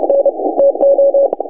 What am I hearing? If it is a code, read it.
Morse code, heard as a keyed tone, about 29 wpm, 560 Hz. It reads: N TO
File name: vk6dxi 160m s50u